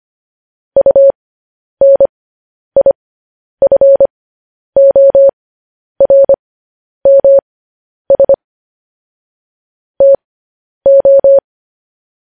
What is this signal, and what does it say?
Morse code, heard as a keyed tone, about 25 wpm, spaced as for 10 wpm, 550 Hz. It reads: UNIFORMS TO
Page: news headlines in morse code at 10 wpm